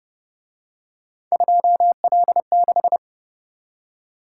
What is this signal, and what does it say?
Morse code, heard as a keyed tone, about 29 wpm, 700 Hz. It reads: 2L6
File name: Speed-Racing - Three Letter-Number Combinations - 25wpm